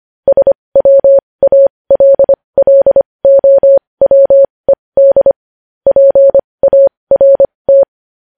Morse code, heard as a keyed tone, about 25 wpm, spaced as for 20 wpm, 550 Hz. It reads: SWALLOWED PART
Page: News headlines in Morse code at 20 wpm